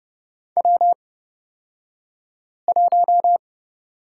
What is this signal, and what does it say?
Morse code, heard as a keyed tone, about 30 wpm, spaced as for 5 wpm, 700 Hz. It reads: W1
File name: Lesson 37d - Letter x - Two Characters 30wpm